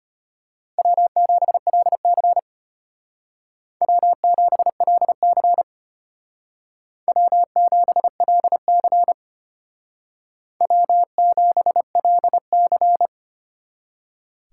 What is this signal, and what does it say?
Morse code, heard as a keyed tone, about 32 wpm, 700 Hz. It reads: W7LC W7LC W7LC W7LC